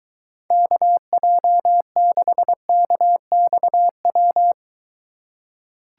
Morse code, heard as a keyed tone, about 23 wpm, 700 Hz. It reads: KJ6KXW